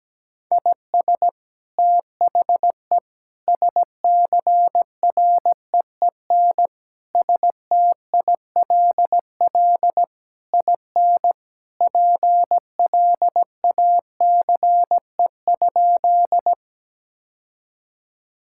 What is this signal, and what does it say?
Morse code, heard as a keyed tone, about 17 wpm, 700 Hz. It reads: IS THE SCREEN STILL IN PLACE?
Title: Sentences from Top 1000 Words - Review Part 6 - 17wpm